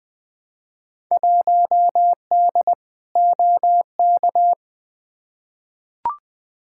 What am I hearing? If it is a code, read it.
Morse code, heard as a keyed tone, about 20 wpm, 700 Hz. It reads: 1D OK
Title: Field Day Exchanges - 20wpm